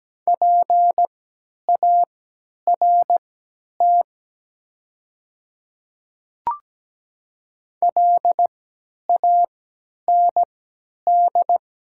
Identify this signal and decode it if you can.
Morse code, heard as a keyed tone, about 17 wpm, spaced as for 10 wpm, 700 Hz. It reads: PART LAND
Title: 3x Character Spacing - Top 200 Words - 17wpm